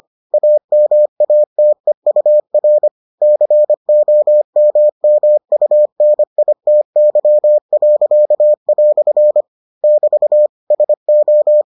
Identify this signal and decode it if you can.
Morse code, heard as a keyed tone, about 25 wpm, 600 Hz. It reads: AMATEUR COMMUNITY." <BT> SO